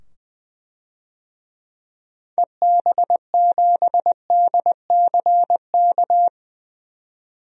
Morse code, heard as a keyed tone, about 20 wpm, 700 Hz. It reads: EB7DCK
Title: DXCC-100 Call Signs - 6 Characters - 20wpm